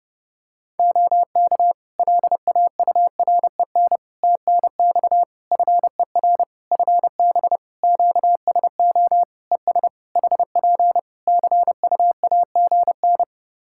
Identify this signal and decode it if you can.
Morse code, heard as a keyed tone, about 29 wpm, 700 Hz. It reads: OK LAUREN TNX FER FB QSO ES HP CUAGN
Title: Practice QSO - Exchange 4 - Ending - 30wpm